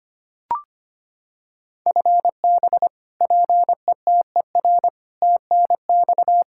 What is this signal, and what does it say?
Morse code, heard as a keyed tone, about 25 wpm, 700 Hz. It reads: FB PETER TNX
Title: Practice QSO - Exchange 3 - More Info - 25wpm